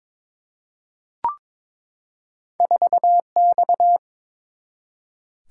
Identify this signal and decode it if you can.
Morse code, heard as a keyed tone, about 22 wpm, 700 Hz. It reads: 4X
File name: practice / 22wpm / Two Letter-Number Combinations - 22wpm